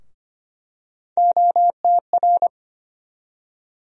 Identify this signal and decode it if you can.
Morse code, heard as a keyed tone, about 25 wpm, 700 Hz. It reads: OTR
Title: USA Call Sign Postfixes - 3 Letters 25wpm